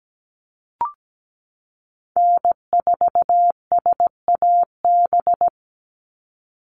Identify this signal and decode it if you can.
Morse code, heard as a keyed tone, about 17 wpm, 700 Hz. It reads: N4SAB
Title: USA Call Signs 17wpm